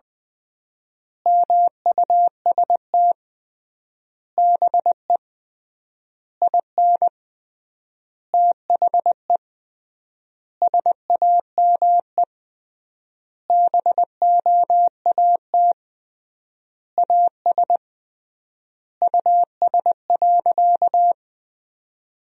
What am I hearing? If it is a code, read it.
Morse code, heard as a keyed tone, about 21 wpm, 700 Hz. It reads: MUST BE IN THE SAME BOAT AS US.
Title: 3x Word-Spacing - Sentences from Top 500 Words - 20wpm